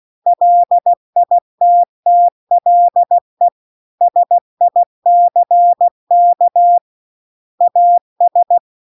Morse code, heard as a keyed tone, about 16 wpm, 700 Hz. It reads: LITTLE SICK AS